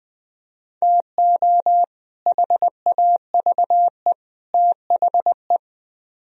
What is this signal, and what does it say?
Morse code, heard as a keyed tone, about 22 wpm, 700 Hz. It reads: TO HAVE THE
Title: Speed-Racing - Sets of 3 Words 20wpm